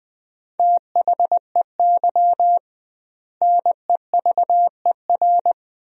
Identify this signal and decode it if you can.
Morse code, heard as a keyed tone, about 20 wpm, 700 Hz. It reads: THEY NEVER
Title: 2x Word-Spacing - Sentences from Top 300 Words - 20wpm